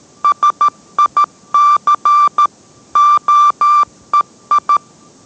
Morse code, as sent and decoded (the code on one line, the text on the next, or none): ... .. -.-. / --- . ..
SIC OEI